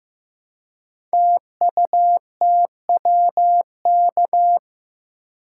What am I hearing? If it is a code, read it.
Morse code, heard as a keyed tone, about 15 wpm, 700 Hz. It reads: TUTWK
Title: Code Groups - 5 Letters - 15wpm